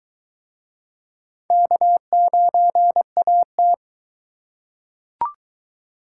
Morse code, heard as a keyed tone, about 23 wpm, 700 Hz. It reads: K9AT